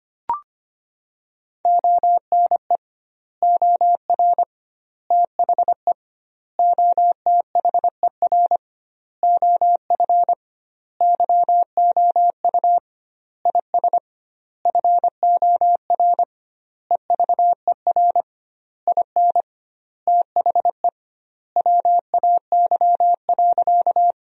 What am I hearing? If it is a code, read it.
Morse code, heard as a keyed tone, about 25 wpm, 700 Hz. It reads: ONE OR THE OTHER OF YOU IS FOR EVER IN THE WAY.
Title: 2x Word-Spacing - Sentences from Top 400 Words - 25wpm